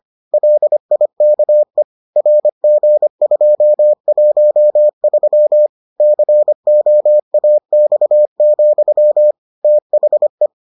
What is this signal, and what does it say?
Morse code, heard as a keyed tone, about 25 wpm, 600 Hz. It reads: LIKE RG213 COAX, THE